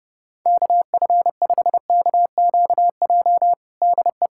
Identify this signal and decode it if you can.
Morse code, heard as a keyed tone, about 32 wpm, 700 Hz. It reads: KF5KQJ DE
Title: USA Callsign DE Callsign 30wpm